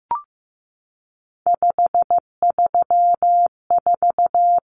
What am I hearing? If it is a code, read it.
Morse code, heard as a keyed tone, about 15 wpm, 700 Hz. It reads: 534
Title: Common Cut Numbers - 3-Digit 15wpm